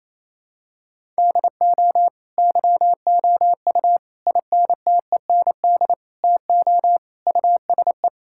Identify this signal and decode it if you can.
Morse code, heard as a keyed tone, about 28 wpm, 700 Hz. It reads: DO YOU INTEND TO USE
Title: Sentences from Top 1600 Words 28wpm